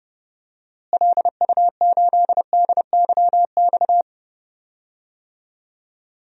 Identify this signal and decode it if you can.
Morse code, heard as a keyed tone, about 29 wpm, 700 Hz. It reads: LU8DYX
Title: DXCC-100 Call Signs - 6 Characters - Repeated 3x - 30wpm